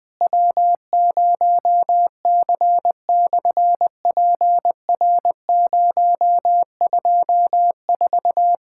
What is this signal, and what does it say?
Morse code, heard as a keyed tone, about 20 wpm, 700 Hz. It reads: W0C/PR024
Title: SOTA QSO Practice 20wpm